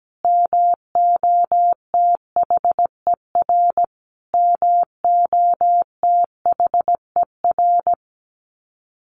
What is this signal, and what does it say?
Morse code, heard as a keyed tone, about 17 wpm, 700 Hz. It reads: MOTHER MOTHER
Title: Top 200 Words Repeated 17wpm